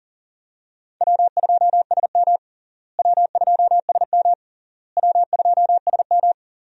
Morse code, heard as a keyed tone, about 40 wpm, 700 Hz. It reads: W2SM W2SM W2SM
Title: USA Call Signs - 4 Characters - Repeated 3x - 40wpm